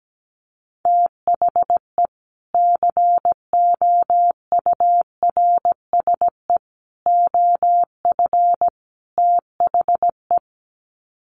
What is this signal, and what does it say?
Morse code, heard as a keyed tone, about 17 wpm, 700 Hz. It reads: THE COURSE OF THE
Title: Sets of 4 Words 17wpm